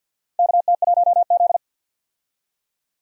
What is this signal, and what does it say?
Morse code, heard as a keyed tone, about 52 wpm, 700 Hz. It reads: KT1Z
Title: USA Call Signs - 4 Characters - 50wpm